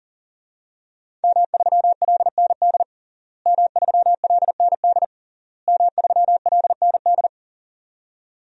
Morse code, heard as a keyed tone, about 40 wpm, 700 Hz. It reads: M3LND M3LND M3LND
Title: DXCC-100 Call Signs - 5 Characters - Repeated 3x - 40wpm